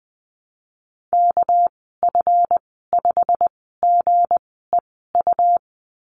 Morse code, heard as a keyed tone, about 20 wpm, spaced as for 15 wpm, 700 Hz. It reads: KF5GEU
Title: 207 Lesson 33g - Letter j - Call Signs 20wpm